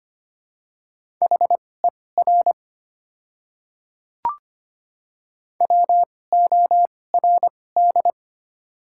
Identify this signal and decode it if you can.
Morse code, heard as a keyed tone, about 25 wpm, spaced as for 18 wpm, 690 Hz. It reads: HER WORD
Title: 2x Character Spacing - Top 300 Words - 25wpm